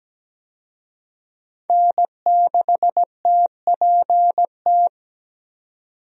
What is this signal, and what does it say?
Morse code, heard as a keyed tone, about 17 wpm, 700 Hz. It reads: N6TPT
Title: USA Call Signs - 5 Characters - 17wpm